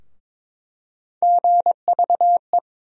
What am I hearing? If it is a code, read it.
Morse code, heard as a keyed tone, about 22 wpm, 700 Hz. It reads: GVE
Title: USA Call Sign Postfixes - 3 Letters 22wpm